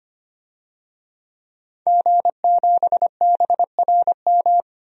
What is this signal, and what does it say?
Morse code, heard as a keyed tone, about 25 wpm, 700 Hz. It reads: G7BRM
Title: DXCC-100 Call Signs - 5 Characters - 25wpm